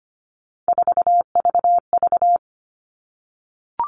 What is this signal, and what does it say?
Morse code, heard as a keyed tone, about 25 wpm, 700 Hz. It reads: 4VV
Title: All Cut Numbers - 3-Digit 25wpm